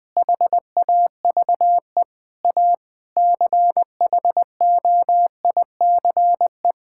Morse code, heard as a keyed tone, about 20 wpm, 700 Hz. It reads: HAVE A CHOICE
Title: Sentences with 5-Word Phrases - 20wpm